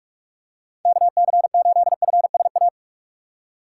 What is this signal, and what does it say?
Morse code, heard as a keyed tone, about 46 wpm, 700 Hz. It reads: KC8FSA
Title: USA Call Signs - 6 Characters - 3x Character Spacing - 45wpm